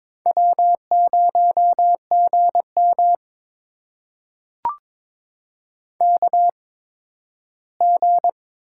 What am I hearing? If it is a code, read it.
Morse code, heard as a keyed tone, about 22 wpm, spaced as for 13 wpm, 700 Hz. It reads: W0GM KG
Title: USA Call Signs - 8x Character Spacing - 22wpm